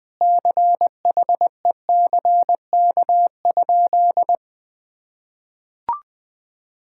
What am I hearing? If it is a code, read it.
Morse code, heard as a keyed tone, about 20 wpm, 700 Hz. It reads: CHECK?
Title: Sentences from Top 1000 Words - Review Part 6 - 20wpm